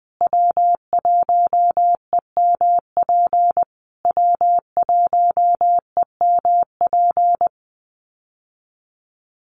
Morse code, heard as a keyed tone, about 20 wpm, 700 Hz. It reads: W1EMP W1EMP
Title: USA Call Signs Repeated 3x 20wpm